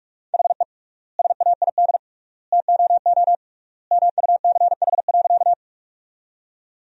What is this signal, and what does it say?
Morse code, heard as a keyed tone, about 46 wpm, 700 Hz. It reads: HE SAID TOO MUCH.
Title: 3x Word-Spacing - Sentences from Top 200 Words - 45wpm